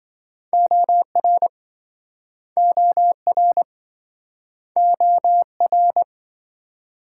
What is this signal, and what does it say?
Morse code, heard as a keyed tone, about 25 wpm, 700 Hz. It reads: OR OR OR